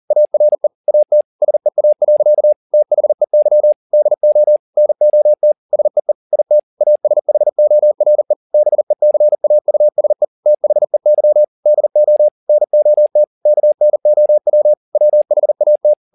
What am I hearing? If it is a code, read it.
Morse code, heard as a keyed tone, about 40 wpm, 600 Hz. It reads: ARE AT SEA. THEY DO NOT SEE IT ASHORE BECAUSE THEY DO NOT KNOW WHAT